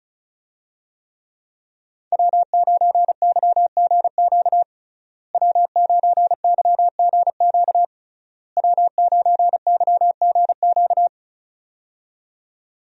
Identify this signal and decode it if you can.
Morse code, heard as a keyed tone, about 36 wpm, 700 Hz. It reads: W9YGQ W9YGQ W9YGQ
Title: USA Call Signs - 5 Characters - Repeated 3x - 35wpm